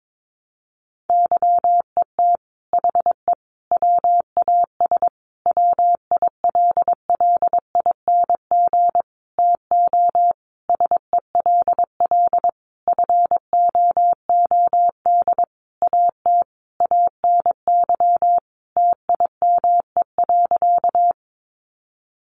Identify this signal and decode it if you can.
Morse code, heard as a keyed tone, about 22 wpm, 700 Hz. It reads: YET HE WAS WILLING TO SELL FOOD AT ANY TIME.
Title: Sentences from Top 1100 Words 22wpm